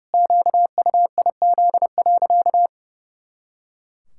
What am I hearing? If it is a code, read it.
Morse code, heard as a keyed tone, about 32 wpm, 700 Hz. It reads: QUIZ.
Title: Pangram Challenge v2 - 30wpm